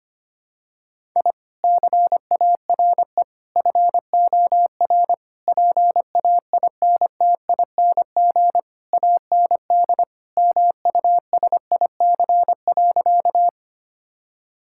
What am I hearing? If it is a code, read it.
Morse code, heard as a keyed tone, about 25 wpm, 700 Hz. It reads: I CARE FOR PAINTING AND MUSIC.
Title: Sentences from Top 1000 Words - Review Part 6 - 25wpm